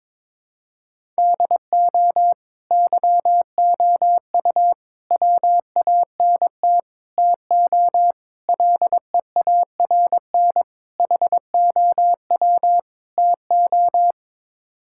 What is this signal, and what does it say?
Morse code, heard as a keyed tone, about 22 wpm, 700 Hz. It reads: DO YOU WANT TO LEARN HOW TO
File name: practice / 22wpm / Sets of 7 Words - 22wpm